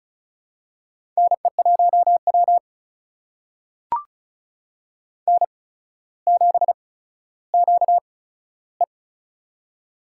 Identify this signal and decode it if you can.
Morse code, heard as a keyed tone, about 36 wpm, spaced as for 15 wpm, 700 Hz. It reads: NE1W N7QE
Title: USA Call Signs - 4 Characters - 8x Character Spacing - 35wpm